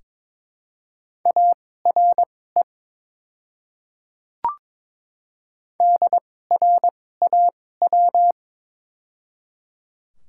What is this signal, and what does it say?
Morse code, heard as a keyed tone, about 23 wpm, spaced as for 16 wpm, 700 Hz. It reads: ARE DRAW
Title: 2x Character Spacing - Top 300 Words - 22wpm